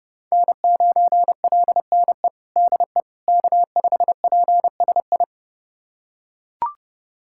Morse code, heard as a keyed tone, about 30 wpm, 700 Hz. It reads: N9LNE DE K5PSI
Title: USA Callsign DE Callsign - 5 Characters - 30wpm